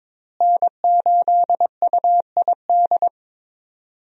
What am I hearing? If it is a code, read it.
Morse code, heard as a keyed tone, about 22 wpm, 700 Hz. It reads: N8UID
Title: USA Call Signs - 5 Characters - Repeated 2x - 22wpm